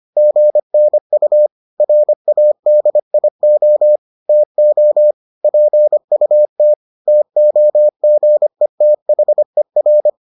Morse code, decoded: GNU RADIO TO PUT TOGETHER

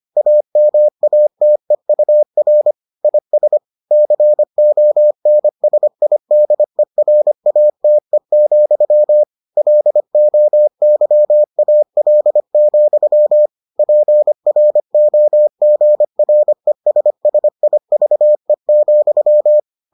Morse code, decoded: AMATEUR IS CONSIDERATE, LOYAL, PROGRESSIVE,

